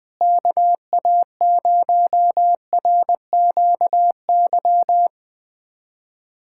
Morse code, heard as a keyed tone, about 20 wpm, 700 Hz. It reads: KA0RQY
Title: USA Callsign DE Callsign - 6 Characters - 20wpm